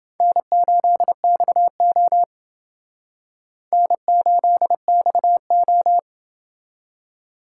Morse code, heard as a keyed tone, about 28 wpm, 700 Hz. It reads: N8XO N8XO